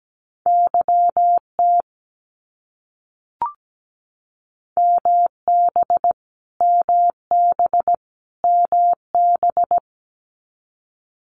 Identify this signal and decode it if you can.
Morse code, heard as a keyed tone, about 17 wpm, 700 Hz. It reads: YT MB MB MB
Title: Canadian Provinces and Territories - Repeated 3x - 17wpm